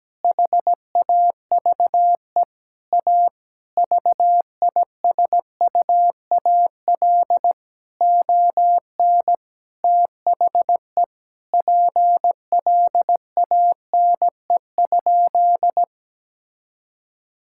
Morse code, heard as a keyed tone, about 17 wpm, 700 Hz. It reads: HAVE A VISUAL ON THE PLANE?
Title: Sentences from Top 1000 Words - Review Part 6 - 17wpm